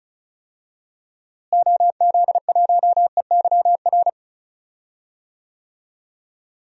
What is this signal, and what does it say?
Morse code, heard as a keyed tone, about 36 wpm, 700 Hz. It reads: OZ1EYR
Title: DXCC-100 Call Signs - 6 Characters - Repeated 2x - 35wpm